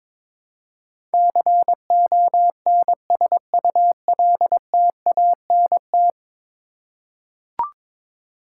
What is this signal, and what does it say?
Morse code, heard as a keyed tone, about 22 wpm, 700 Hz. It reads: CONSULTANT
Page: morse-code-ninja-practice-22wpm